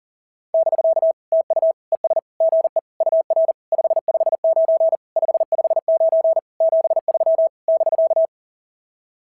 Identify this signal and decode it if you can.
Morse code, heard as a keyed tone, about 40 wpm, 650 Hz. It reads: <BK> TU ES GE UR 559 559 73 <BK>
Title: SOTA QSO Practice 40wpm